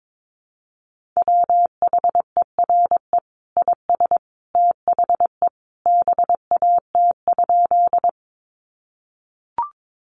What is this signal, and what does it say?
Morse code, heard as a keyed tone, about 22 wpm, 700 Hz. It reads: WHERE IS THE BAT?
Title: Sentences from Top 1500 Words 22wpm